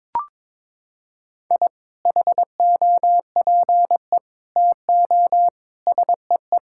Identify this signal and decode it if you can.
Morse code, heard as a keyed tone, about 22 wpm, 700 Hz. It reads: I HOPE TO SEE
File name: practice / 22wpm / Sentences with 2-Word Phrases - 22wpm